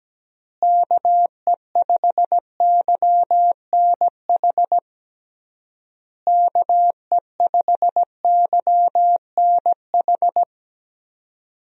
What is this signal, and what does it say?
Morse code, heard as a keyed tone, about 17 wpm, 700 Hz. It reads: KE5YNH KE5YNH
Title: USA Call Signs - 6 Characters - Repeated 3x - 17wpm